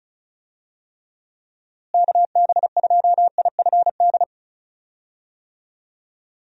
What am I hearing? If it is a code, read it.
Morse code, heard as a keyed tone, about 36 wpm, 700 Hz. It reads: KB2IFD
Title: USA Call Signs - 6 Characters - Repeated 2x - 35wpm